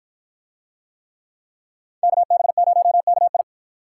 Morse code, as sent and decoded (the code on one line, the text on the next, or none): -.- -... ----- -.- ..
KB0KI